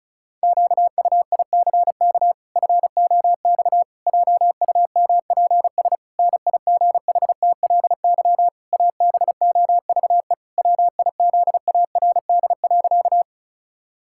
Morse code, heard as a keyed tone, about 36 wpm, 700 Hz. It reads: QUICK FOX JUMPS NIGHTLY ABOVE WIZARD.
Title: Pangram Challenge v2 - 35wpm